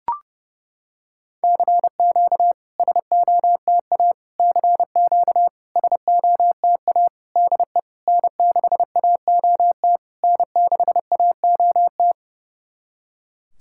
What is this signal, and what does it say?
Morse code, heard as a keyed tone, about 30 wpm, 700 Hz. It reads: CQ SOTA CQ SOTA DE N6AOT N6AOT
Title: SOTA QSO Practice 30wpm